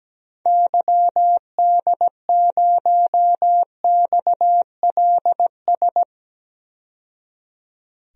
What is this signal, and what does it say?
Morse code, heard as a keyed tone, about 17 wpm, 700 Hz. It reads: YD0XLS
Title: DXCC-100 Call Signs - 6 Characters - 17wpm